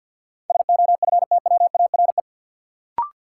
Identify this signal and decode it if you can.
Morse code, heard as a keyed tone, about 53 wpm, 700 Hz. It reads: SOFTWARE
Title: Sentences from Top 1000 Words - Review Part 3 - New Word Drill - 50wpm